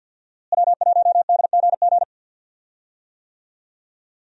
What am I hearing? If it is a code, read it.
Morse code, heard as a keyed tone, about 51 wpm, 690 Hz. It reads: W1DGG
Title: USA Call Signs - 5 Characters - Repeated 3x - 50wpm